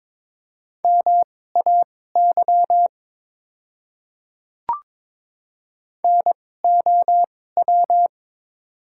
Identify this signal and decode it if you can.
Morse code, heard as a keyed tone, about 22 wpm, spaced as for 16 wpm, 690 Hz. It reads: MAY NOW
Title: 2x Character Spacing - Top 200 Words - 22wpm